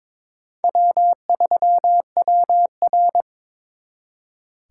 Morse code, heard as a keyed tone, about 26 wpm, 700 Hz. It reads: W3WR
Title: Speed Racing - USA 1x2 Call Signs 22wpm